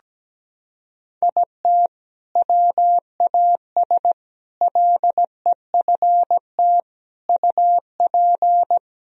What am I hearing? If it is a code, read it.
Morse code, heard as a keyed tone, about 17 wpm, 700 Hz. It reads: IT WAS LEFT UP
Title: Sentences from Top 1000 Words - Review Part 4 - 17wpm